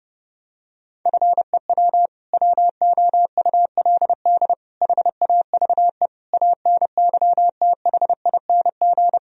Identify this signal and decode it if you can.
Morse code, heard as a keyed tone, about 30 wpm, 700 Hz. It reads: FEW WOULD HAVE ANYTHING
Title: Sentences from Top 1000 Words - Review Part 1 - 30wpm